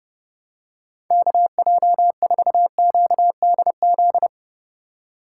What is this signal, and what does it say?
Morse code, heard as a keyed tone, about 32 wpm, 700 Hz. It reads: KJ4QDZ
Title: USA Call Signs - 4x Character Spacing - 30wpm